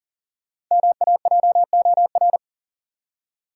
Morse code, heard as a keyed tone, about 38 wpm, 700 Hz. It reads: MAJOR
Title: Five Letter Words - 40wpm